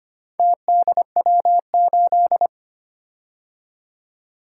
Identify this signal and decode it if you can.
Morse code, heard as a keyed tone, about 25 wpm, 700 Hz. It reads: TDW8
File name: Four Letter-Number Combinations - 25wpm